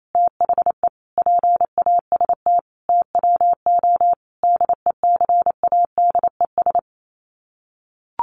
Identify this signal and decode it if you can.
Morse code, heard as a keyed tone, about 28 wpm, 700 Hz. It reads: THE PAST TWO DECADES
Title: Sets of 5 Words Encore2 28wpm